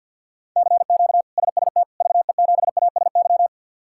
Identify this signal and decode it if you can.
Morse code, heard as a keyed tone, about 51 wpm, 700 Hz. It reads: CQ SST VE7ASY
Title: K1USN SST QSO Practice - 50wpm